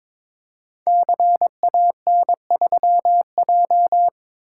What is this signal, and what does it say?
Morse code, heard as a keyed tone, about 22 wpm, 700 Hz. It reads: CAN3J
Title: Code Groups - 5 Letter:Numbers - 22wpm